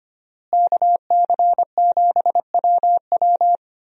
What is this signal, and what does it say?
Morse code, heard as a keyed tone, about 25 wpm, 700 Hz. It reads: KC7WW
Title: USA Call Signs - 5 Characters - Repeated 3x - 25wpm